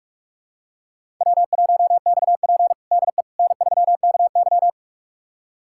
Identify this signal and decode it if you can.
Morse code, heard as a keyed tone, about 46 wpm, 700 Hz. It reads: W1XP DE N3KY
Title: USA Callsign DE Callsign - 4 Characters - 45wpm